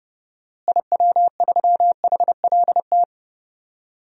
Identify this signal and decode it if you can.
Morse code, heard as a keyed tone, about 30 wpm, 700 Hz. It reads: IW3HLT
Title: DXCC-100 Call Signs - 6 Characters - Repeated 3x - 30wpm